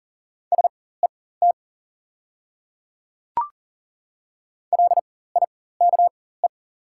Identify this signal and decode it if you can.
Morse code, heard as a keyed tone, about 40 wpm, spaced as for 19 wpm, 690 Hz. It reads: SET LIKE
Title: 4x Character Spacing - Top 500 Words - 40wpm